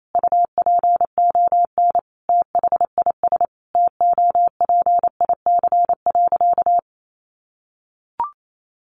Morse code, heard as a keyed tone, about 28 wpm, 700 Hz. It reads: UPON THIS TOPIC.
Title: Sentences from Top 1200 Words 28wpm